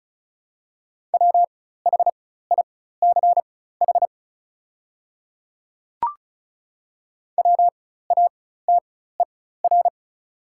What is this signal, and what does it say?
Morse code, heard as a keyed tone, about 36 wpm, spaced as for 16 wpm, 700 Hz. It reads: WHICH WATER
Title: 4x Character Spacing - Top 200 Words - 35wpm